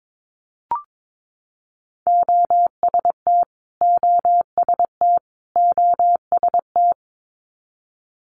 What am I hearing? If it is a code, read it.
Morse code, heard as a keyed tone, about 22 wpm, 700 Hz. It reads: OST OST OST
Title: Multi-letter Phonemes Repeated 3x 22wpm